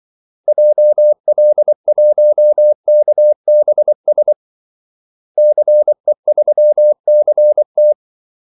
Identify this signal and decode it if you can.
Morse code, heard as a keyed tone, about 24 wpm, 600 Hz. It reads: JL1KBS CE3CT
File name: Callsigns-24wpm